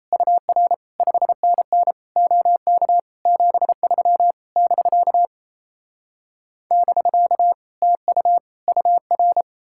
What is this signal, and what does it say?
Morse code, heard as a keyed tone, about 38 wpm, 700 Hz. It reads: UR 5NN OK 73 <BK> <BK> TU UR